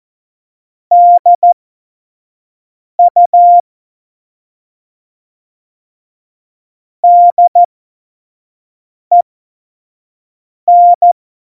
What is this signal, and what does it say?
Morse code, heard as a keyed tone, about 14 wpm, spaced as for 5 wpm, 700 Hz. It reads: DU DEN
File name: Morsekurs / Witze / Witz 17. November 2020, 25 F70